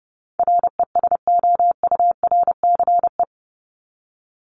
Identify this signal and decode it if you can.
Morse code, heard as a keyed tone, about 30 wpm, 700 Hz. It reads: RESOURCE
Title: Top 1001-1250 Words 30wpm